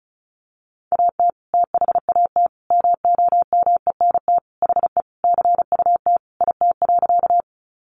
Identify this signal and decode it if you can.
Morse code, heard as a keyed tone, about 34 wpm, 700 Hz. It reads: AT THAT MOMENT HE CUT IT.
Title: Sentences from Top 1000 Words 35wpm